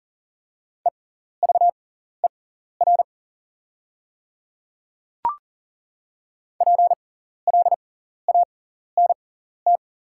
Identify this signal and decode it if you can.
Morse code, heard as a keyed tone, about 40 wpm, spaced as for 14 wpm, 690 Hz. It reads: EVER PLANT
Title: 6x Character Spacing - Top 500 Words - 40wpm